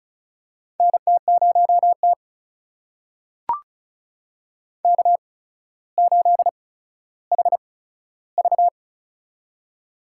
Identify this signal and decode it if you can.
Morse code, heard as a keyed tone, about 36 wpm, spaced as for 15 wpm, 700 Hz. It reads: NT0T K8HV
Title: USA Call Signs - 4 Characters - 8x Character Spacing - 35wpm